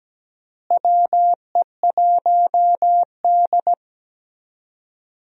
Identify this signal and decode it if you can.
Morse code, heard as a keyed tone, about 17 wpm, 700 Hz. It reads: WE1D
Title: USA Call Signs - 4 Characters - Repeated 3x - 17wpm